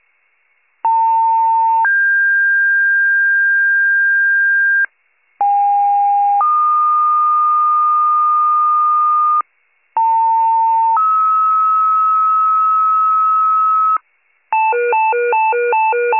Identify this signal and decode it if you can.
Two-tone page, heard as a one-second tone, then a three-second tone, about 900 then 1600 Hz, about 810 then 1220 Hz, about 900 then 1290 Hz. It is heard from 0.8 to 14.0 s.